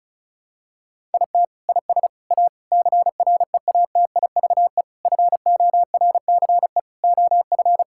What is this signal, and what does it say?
Morse code, heard as a keyed tone, about 34 wpm, 700 Hz. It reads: IT IS A CREATIVE FORCE OF
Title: Sentences from Top 1000 Words - Review Part 5 - 35wpm